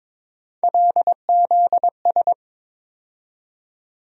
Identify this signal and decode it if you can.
Morse code, heard as a keyed tone, about 22 wpm, 700 Hz. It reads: LZS
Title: USA Call Sign Postfixes - 3 Letters 22wpm